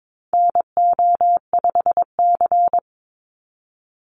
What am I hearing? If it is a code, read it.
Morse code, heard as a keyed tone, about 22 wpm, 700 Hz. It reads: NO5C
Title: USA Call Signs 22wpm